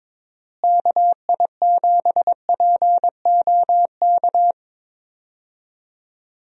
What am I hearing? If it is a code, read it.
Morse code, heard as a keyed tone, about 22 wpm, 700 Hz. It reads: KI7POK
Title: USA Call Signs - 6 Characters - Repeated 3x - 22wpm